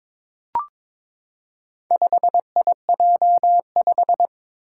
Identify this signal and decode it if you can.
Morse code, heard as a keyed tone, about 22 wpm, 700 Hz. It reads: 5IJ5
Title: Four Challenging Letter-Number Combinations 22wpm